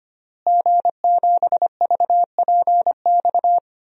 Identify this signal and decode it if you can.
Morse code, heard as a keyed tone, about 25 wpm, 700 Hz. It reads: G7VPX
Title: DXCC-100 Call Signs - 5 Characters - Repeated 3x - 25wpm